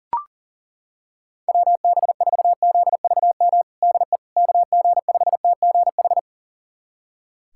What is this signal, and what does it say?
Morse code, heard as a keyed tone, about 40 wpm, 700 Hz. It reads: WB4ZVM DE KG5TGH
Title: USA Callsign DE Callsign - 6 Characters - 40wpm